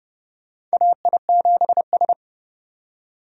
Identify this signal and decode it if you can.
Morse code, heard as a keyed tone, about 30 wpm, 700 Hz. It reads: AI7S
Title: USA Call Signs - 4 Characters - 30wpm